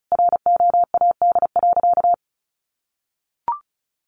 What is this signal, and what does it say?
Morse code, heard as a keyed tone, about 34 wpm, 700 Hz. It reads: ROAD.
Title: Sentences from Top 1500 Words 35wpm